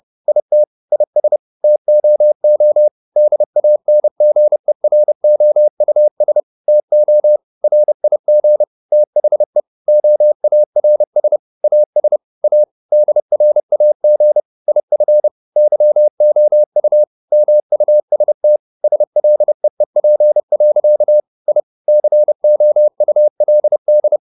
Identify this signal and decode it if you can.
Morse code, heard as a keyed tone, about 30 wpm, 600 Hz. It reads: IT IS TOO DANGEROUS TO RIG THE OARS AS A DRAG IF YOU MUST SLEEP. I COULD